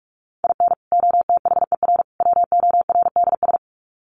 Morse code, heard as a keyed tone, about 44 wpm, 700 Hz. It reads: IN OTHER WORDS
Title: Sets of 3 Words 45wpm